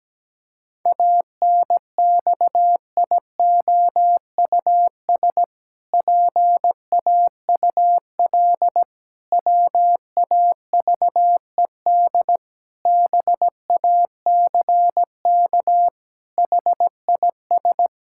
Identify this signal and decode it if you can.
Morse code, heard as a keyed tone, about 17 wpm, 700 Hz. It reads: ANXIOUS PAUL WAVED BACK HIS
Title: Pangram Challenge v2 - 17wpm